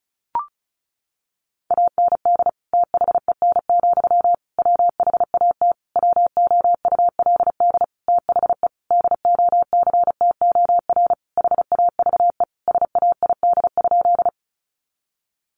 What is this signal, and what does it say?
Morse code, heard as a keyed tone, about 34 wpm, 700 Hz. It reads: AND THEN, WHAT WOULD THE DOCTOR HAVE SAID?
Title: Sentences from Top 1500 Words 35wpm